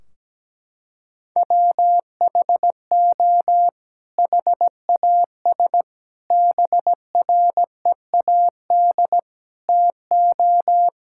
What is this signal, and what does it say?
Morse code, heard as a keyed tone, about 17 wpm, 700 Hz. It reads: WHO HAS BREAD TO
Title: Sentences from Top 1600 Words 17wpm